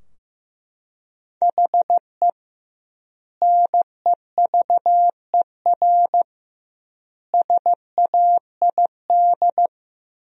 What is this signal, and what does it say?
Morse code, heard as a keyed tone, about 15 wpm, 700 Hz. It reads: HE NEVER SAID
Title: 2x Word-Spacing - Sentences from Top 500 Words - 15wpm